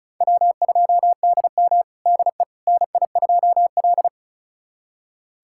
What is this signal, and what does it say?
Morse code, heard as a keyed tone, about 36 wpm, 700 Hz. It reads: W2DM DE NI2L
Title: USA Callsign DE Callsign - 4 Characters - 35wpm